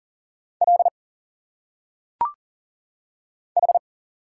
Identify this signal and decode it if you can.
Morse code, heard as a keyed tone, about 40 wpm, 700 Hz. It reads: L H
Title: Single Letters - Phonetic Pronunciation 40wpm